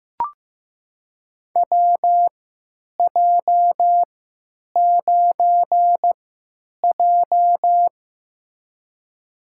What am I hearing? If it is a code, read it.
Morse code, heard as a keyed tone, about 15 wpm, spaced as for 9 wpm, 700 Hz. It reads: WJ9J
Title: USA Call Signs - 4 Characters - 3x Character Spacing - 15wpm